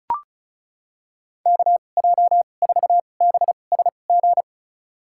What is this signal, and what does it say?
Morse code, heard as a keyed tone, about 36 wpm, spaced as for 25 wpm, 700 Hz. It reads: KJ4BSG
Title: USA Call Signs - 6 Characters - 2x Character Spacing - 35wpm